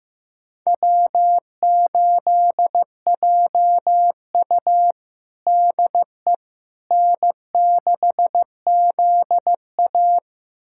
Morse code, heard as a keyed tone, about 15 wpm, 700 Hz. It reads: W8JU DE N6ZA
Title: USA Callsign DE Callsign - 4 Characters - 15wpm